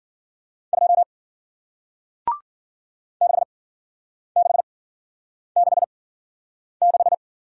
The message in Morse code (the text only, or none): 2 6 6 6 6